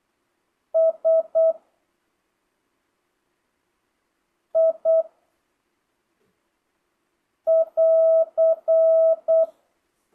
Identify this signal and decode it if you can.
Morse code, heard as a keyed tone, about 8 wpm, 650 Hz. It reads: S I <AR>
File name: jh8cbh / 2025cw